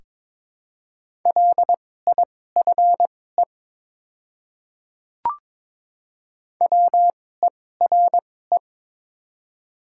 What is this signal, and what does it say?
Morse code, heard as a keyed tone, about 22 wpm, spaced as for 16 wpm, 700 Hz. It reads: LIFE WERE
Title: 2x Character Spacing - Top 500 Words - 22wpm